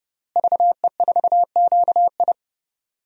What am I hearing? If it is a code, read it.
Morse code, heard as a keyed tone, about 30 wpm, 700 Hz. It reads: VE4QI
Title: DXCC-100 Call Signs - 5 Characters - Repeated 3x - 30wpm